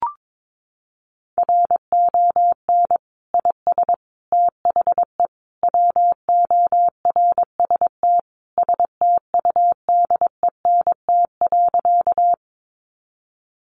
Morse code, heard as a keyed tone, about 22 wpm, 700 Hz. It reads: RON IS THE WORST STUDENT.
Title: Additional Easy Phrases 22wpm